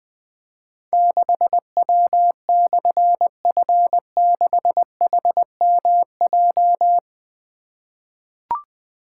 Morse code, heard as a keyed tone, about 20 wpm, 700 Hz. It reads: 6W/F6HMJ
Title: Difficult Call Signs 20wpm